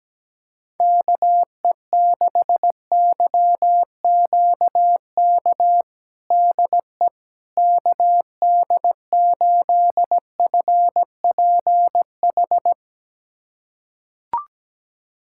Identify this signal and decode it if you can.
Morse code, heard as a keyed tone, about 17 wpm, 700 Hz. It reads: KE6YQK DE KD8FPH
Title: USA Callsign DE Callsign 17wpm